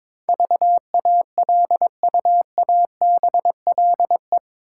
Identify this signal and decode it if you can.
Morse code, heard as a keyed tone, about 22 wpm, 700 Hz. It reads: VALUABLE
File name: practice / 22wpm / Eight Letter Words - 22wpm